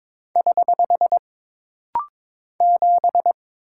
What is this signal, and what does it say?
Morse code, heard as a keyed tone, about 22 wpm, 700 Hz. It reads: <HH> 7
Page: This rapid-fire timing is geared to push your ICR (Instant Character Recognition) skills.